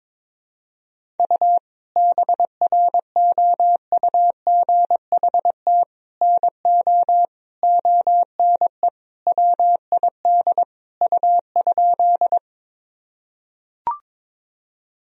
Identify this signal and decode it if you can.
Morse code, heard as a keyed tone, about 21 wpm, 700 Hz. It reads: U BROUGHT NO ONE WID U?
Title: Sentences from Top 500 Words and Abbreviations 22wpm